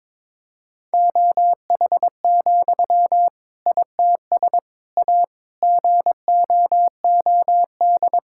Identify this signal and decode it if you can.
Morse code, heard as a keyed tone, about 21 wpm, 700 Hz. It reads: OH, ITS A GOOD
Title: Sentences from Top 1000 Words - Review Part 5 - 22wpm